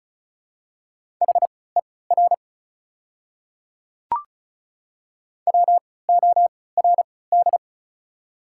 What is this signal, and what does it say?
Morse code, heard as a keyed tone, about 36 wpm, spaced as for 20 wpm, 700 Hz. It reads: HER WORD
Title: 3x Character Spacing - Top 100 Words - 35wpm